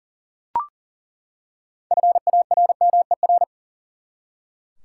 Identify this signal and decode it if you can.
Morse code, heard as a keyed tone, about 40 wpm, 700 Hz. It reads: FARMER
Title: Six Letter Words - 40wpm